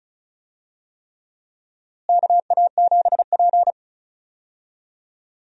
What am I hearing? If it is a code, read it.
Morse code, heard as a keyed tone, about 36 wpm, 700 Hz. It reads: KA7P
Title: USA Call Signs - 4 Characters - Repeated 3x - 35wpm